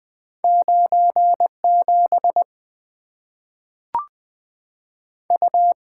Morse code, heard as a keyed tone, about 20 wpm, 700 Hz. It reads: 9 7 U
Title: Two Letter-Number Combinations - 20wpm